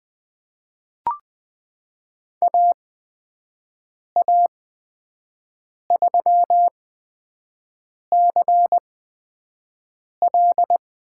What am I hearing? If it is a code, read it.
Morse code, heard as a keyed tone, about 20 wpm, spaced as for 5 wpm, 700 Hz. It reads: AA3CL
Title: USA Call Signs - 5 Characters - 8x Character Spacing - 20wpm